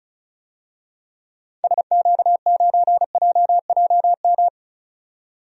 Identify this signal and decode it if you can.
Morse code, heard as a keyed tone, about 36 wpm, 700 Hz. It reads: SQ9JJM
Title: DXCC-100 Call Signs - 6 Characters - 35wpm